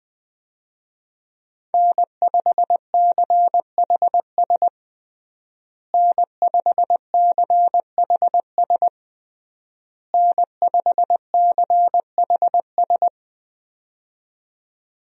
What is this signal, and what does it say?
Morse code, heard as a keyed tone, about 21 wpm, 700 Hz. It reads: N5CHS N5CHS N5CHS
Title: USA Call Signs - 5 Characters - Repeated 3x - 20wpm